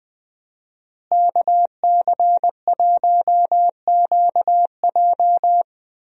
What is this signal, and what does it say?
Morse code, heard as a keyed tone, about 21 wpm, 700 Hz. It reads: KC1QJ
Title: USA Call Signs - 5 Characters - Repeated 2x - 20wpm